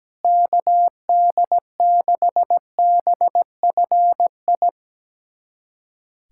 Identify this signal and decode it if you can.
Morse code, heard as a keyed tone, about 17 wpm, 700 Hz. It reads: KD6BFI
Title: USA Callsign DE Callsign 17wpm